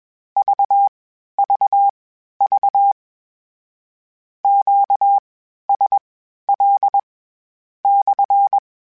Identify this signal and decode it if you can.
Morse code, heard as a keyed tone, about 21 wpm, spaced as for 11 wpm, 800 Hz. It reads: VVV QSL/
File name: Q Calls / Q Calls-mp3s / Q Calls SPSL1 @20